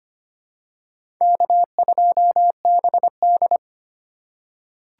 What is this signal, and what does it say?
Morse code, heard as a keyed tone, about 25 wpm, 700 Hz. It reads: K2BD
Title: USA Call Signs - 4 Characters - 4x Character Spacing - 25wpm